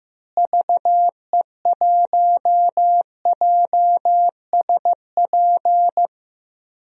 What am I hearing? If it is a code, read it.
Morse code, heard as a keyed tone, about 15 wpm, 700 Hz. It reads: VE1JSP
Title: DXCC-100 Call Signs - 6 Characters - Repeated 2x - 15wpm